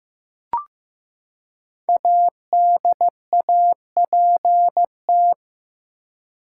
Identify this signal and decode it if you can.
Morse code, heard as a keyed tone, about 15 wpm, 700 Hz. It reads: ADAPT
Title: Five Letter Words - 15wpm